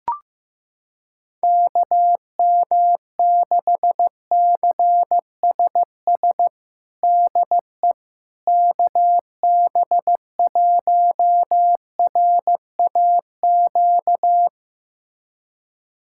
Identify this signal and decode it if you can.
Morse code, heard as a keyed tone, about 15 wpm, 700 Hz. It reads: KM6CSS DE KB1RAQ
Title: USA Callsign DE Callsign 15wpm